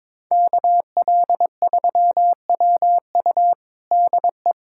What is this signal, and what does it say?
Morse code, heard as a keyed tone, about 22 wpm, 700 Hz. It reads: KL3WU DE
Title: USA Callsign DE Callsign 22wpm